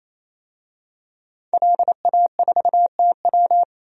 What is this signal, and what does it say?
Morse code, heard as a keyed tone, about 28 wpm, 700 Hz. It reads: LA4TW
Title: DXCC-100 Call Signs - 5 Characters - 28wpm